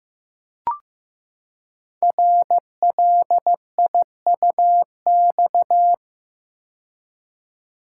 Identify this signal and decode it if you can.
Morse code, heard as a keyed tone, about 15 wpm, 700 Hz. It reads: RLIUX
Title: Code Groups - 5 Letter:Numbers - 15wpm